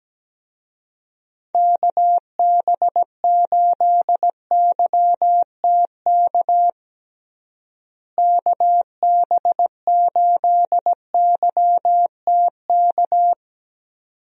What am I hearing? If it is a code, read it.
Morse code, heard as a keyed tone, about 17 wpm, 700 Hz. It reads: KB8YTK KB8YTK
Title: USA Call Signs - 6 Characters - Repeated 2x - 17wpm